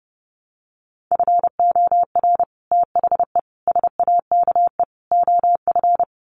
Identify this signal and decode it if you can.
Morse code, heard as a keyed tone, about 30 wpm, 700 Hz. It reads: FOR THE SAKE OF
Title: Sets of 4 Words 30wpm